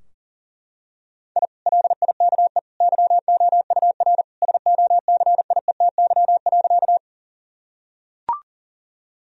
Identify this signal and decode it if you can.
Morse code, heard as a keyed tone, about 37 wpm, 700 Hz. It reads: I LIKE YOUR SOCIETY.
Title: Sentences from Top 1000 Words - Review Part 4 - 40wpm